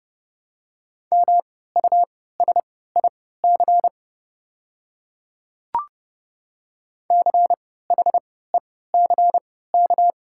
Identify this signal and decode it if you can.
Morse code, heard as a keyed tone, about 30 wpm, spaced as for 17 wpm, 700 Hz. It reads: MUSIC CHECK
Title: 3x Character Spacing - Top 500 Words - 30wpm